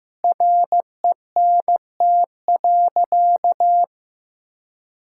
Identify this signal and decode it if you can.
Morse code, heard as a keyed tone, about 15 wpm, 700 Hz. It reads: RENT.
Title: Sentences from Top 1000 Words - Review Part 5 - 15wpm